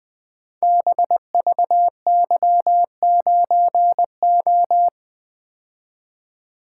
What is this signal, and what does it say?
Morse code, heard as a keyed tone, about 20 wpm, 700 Hz. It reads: BVY9O
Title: Code Groups - 5 Letter:Numbers - 20wpm